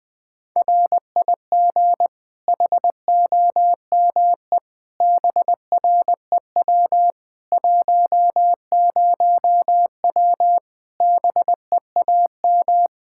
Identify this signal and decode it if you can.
Morse code, heard as a keyed tone, about 20 wpm, 700 Hz. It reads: RIG HOME BREW 10W BEAM
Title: Practice QSO - Exchange 2 - Info - 20wpm